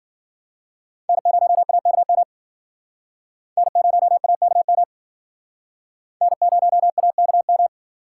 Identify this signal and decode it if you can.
Morse code, heard as a keyed tone, about 50 wpm, 700 Hz. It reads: N0AKM N0AKM N0AKM